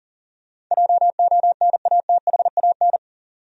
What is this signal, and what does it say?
Morse code, heard as a keyed tone, about 40 wpm, 700 Hz. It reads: JONATHAN
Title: USA Common Names - 40wpm